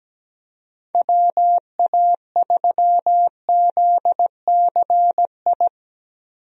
Morse code, heard as a keyed tone, about 17 wpm, 700 Hz. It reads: WA3ZCI
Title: USA Call Signs - 6 Characters - 17wpm